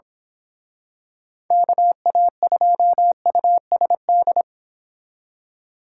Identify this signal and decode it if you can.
Morse code, heard as a keyed tone, about 26 wpm, 700 Hz. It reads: KA2USD
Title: Speed Racing - USA 2x3 Call Signs 22wpm